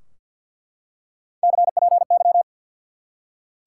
Morse code, heard as a keyed tone, about 48 wpm, 700 Hz. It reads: CPY
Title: Top 180 QSO Elements 50wpm